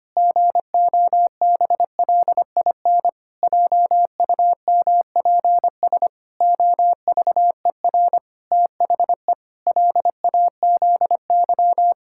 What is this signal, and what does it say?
Morse code, heard as a keyed tone, about 25 wpm, 700 Hz. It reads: GOBLIN JUMPS OVER THE LAZY
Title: Pangram Challenge v2 - 25wpm